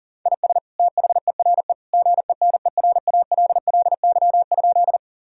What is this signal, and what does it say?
Morse code, heard as a keyed tone, about 40 wpm, 700 Hz. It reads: IS THERE GENERALLY?
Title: Sentences from Top 1000 Words - Review Part 4 - 40wpm